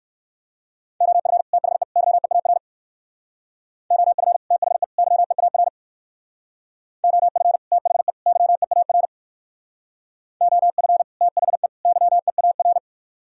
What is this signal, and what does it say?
Morse code, heard as a keyed tone, about 58 wpm, 700 Hz. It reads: OF THE YEAR OF THE YEAR OF THE YEAR OF THE YEAR